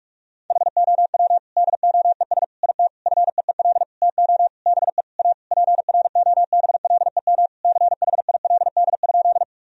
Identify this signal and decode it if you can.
Morse code, heard as a keyed tone, about 42 wpm, 700 Hz. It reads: HOW DOES IT FEEL TO BE A PROBLEM CHILD?
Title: Sentences from Top 1000 Words - Review Part 2 - 45wpm